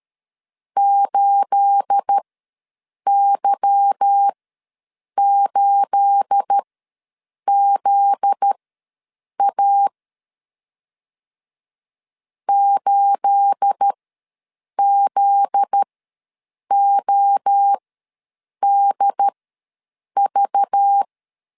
Morse code, heard as a keyed tone, about 13 wpm, spaced as for 7 wpm, 800 Hz. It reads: 8Y8ZA 8ZODV
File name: LEZIONI / 9 28 ws 12wpm 3354